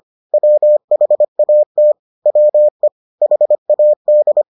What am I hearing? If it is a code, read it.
Morse code, heard as a keyed tone, about 25 wpm, 600 Hz. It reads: WHAT WE HAD